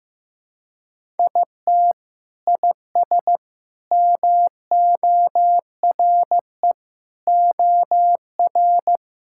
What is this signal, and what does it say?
Morse code, heard as a keyed tone, about 15 wpm, 700 Hz. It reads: IT IS MORE OR
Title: Sentences from Top 1000 Words - Review Part 6 - 15wpm